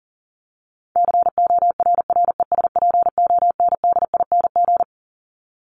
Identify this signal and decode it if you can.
Morse code, heard as a keyed tone, about 40 wpm, 700 Hz. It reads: CORRESPONDING
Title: ING Suffix - Top 100 Words with Base Words 40wpm